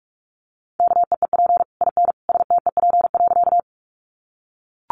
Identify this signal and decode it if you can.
Morse code, heard as a keyed tone, about 44 wpm, 700 Hz. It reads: KEEP IN STEP.
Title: Sentences from Top 400 Words 45wpm